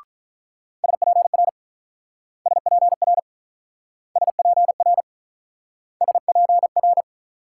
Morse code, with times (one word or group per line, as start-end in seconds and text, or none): 0.8-1.5 SPR
2.5-3.2 SPR
4.2-5.0 SPR
6.0-7.0 SPR